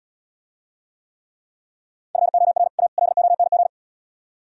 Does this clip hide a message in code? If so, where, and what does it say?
Yes: Morse code from 2.2 to 3.7 s, heard as a keyed tone, about 79 wpm, 700 Hz. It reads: FOR A LONG